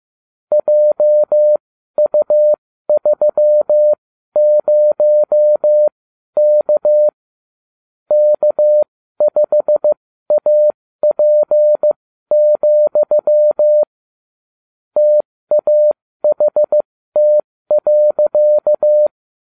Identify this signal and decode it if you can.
Morse code, heard as a keyed tone, about 15 wpm, spaced as for 12 wpm, 600 Hz. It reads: JU30K K5AP, TAHT.